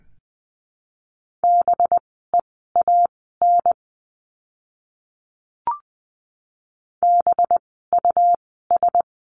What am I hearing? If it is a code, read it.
Morse code, heard as a keyed tone, about 20 wpm, spaced as for 14 wpm, 700 Hz. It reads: BEAN BUS
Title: 185 Lesson 30f - Letter b - Words 20wpm